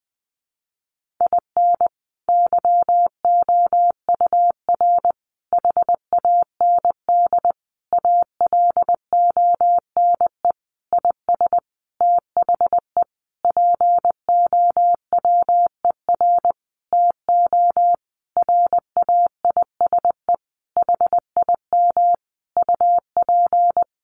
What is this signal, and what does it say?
Morse code, heard as a keyed tone, about 20 wpm, 700 Hz. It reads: IN YOUR HAND ALONE IS THE POWER TO RAISE HIM UP
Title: Sentences from Top 1400 Words 20wpm